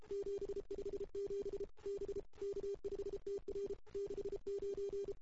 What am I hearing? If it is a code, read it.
Morse code, heard as a keyed tone, about 32 wpm, 400 Hz. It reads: 757 B K5TR 69
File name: CW / 2008-ss-cw-k5tr-op-n5rz -05:00